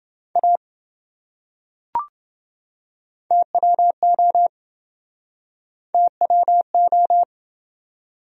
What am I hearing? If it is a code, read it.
Morse code, heard as a keyed tone, about 27 wpm, 700 Hz. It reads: A TWO TWO